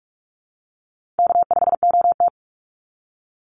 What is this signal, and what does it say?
Morse code, heard as a keyed tone, about 44 wpm, 700 Hz. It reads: K5OT
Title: Easy Call Signs Repeated 2x - 45wpm